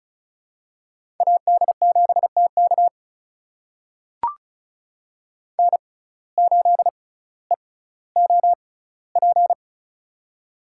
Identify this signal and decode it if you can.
Morse code, heard as a keyed tone, about 36 wpm, spaced as for 18 wpm, 700 Hz. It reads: AD7TK N8EOP
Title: USA Call Signs - 5 Characters - 6x Character Spacing - 35wpm